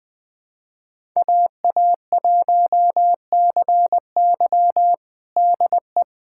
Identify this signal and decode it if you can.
Morse code, heard as a keyed tone, about 20 wpm, 700 Hz. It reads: AA1CY DE
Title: USA Callsign DE Callsign - 5 Characters - 20wpm